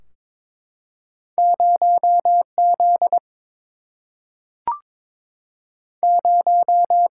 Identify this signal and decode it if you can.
Morse code, heard as a keyed tone, about 22 wpm, 700 Hz. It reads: 0 Z 0
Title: Single-Letter Single-Number 22wpm